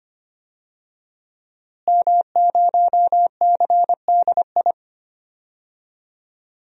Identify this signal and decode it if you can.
Morse code, heard as a keyed tone, about 25 wpm, 700 Hz. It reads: M0CDI
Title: DXCC-100 Call Signs - 5 Characters - Repeated 3x - 25wpm